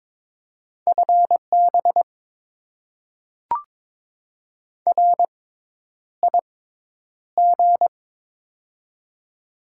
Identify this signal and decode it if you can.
Morse code, heard as a keyed tone, about 22 wpm, spaced as for 10 wpm, 690 Hz. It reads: FB RIG
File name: Top 20 QSO Elements - 6x Character Spacing - 22wpm